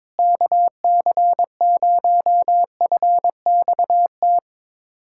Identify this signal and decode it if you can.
Morse code, heard as a keyed tone, about 22 wpm, 700 Hz. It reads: KC0FXT
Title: USA Callsign DE Callsign - 6 Characters - 22wpm